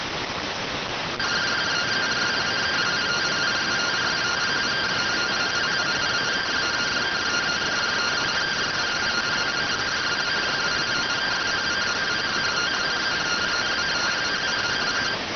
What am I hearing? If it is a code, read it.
Two-tone teleprinter signal, about 170 Hz shift.